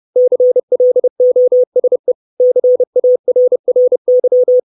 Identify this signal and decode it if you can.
Morse code, heard as a keyed tone, about 30 wpm, 500 Hz. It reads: CLOSE CARRY
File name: 632 Words 30WPM